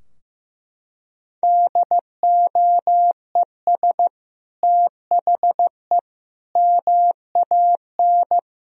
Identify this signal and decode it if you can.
Morse code, heard as a keyed tone, about 15 wpm, 700 Hz. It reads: DOES THE MAN
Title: Sentences from Top 1000 Words - Review Part 4 - 15wpm